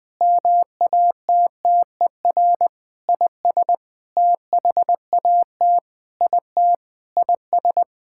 Morse code, heard as a keyed tone, about 20 wpm, 700 Hz. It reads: MATTER IS THAT IT IS
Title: Sentences with 7-Word Phrases - 20wpm